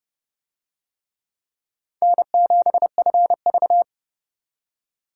Morse code, heard as a keyed tone, about 26 wpm, 700 Hz. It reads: N7FV